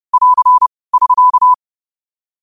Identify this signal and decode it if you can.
Morse code, heard as a keyed tone, about 30 wpm, 1000 Hz. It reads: <AR> <SK>